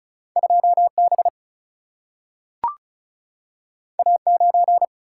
Morse code, heard as a keyed tone, about 36 wpm, 700 Hz. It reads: 2B A9
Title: Single-Letter Single-Number 35wpm